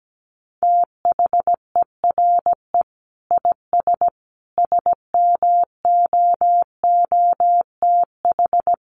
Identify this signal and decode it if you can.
Morse code, heard as a keyed tone, about 17 wpm, 700 Hz. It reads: THERE IS SMOOTH
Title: Sentences from Top 1400 Words 17wpm